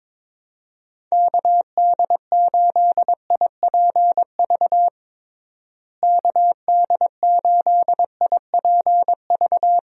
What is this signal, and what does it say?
Morse code, heard as a keyed tone, about 22 wpm, 700 Hz. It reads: KD8IPV KD8IPV
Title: USA Call Signs - 6 Characters - Repeated 2x - 22wpm